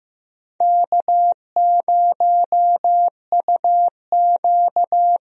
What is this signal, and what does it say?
Morse code, heard as a keyed tone, about 16 wpm, 700 Hz. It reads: K0UQ
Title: USA Call Signs - 4 Characters - Repeated 3x - 15wpm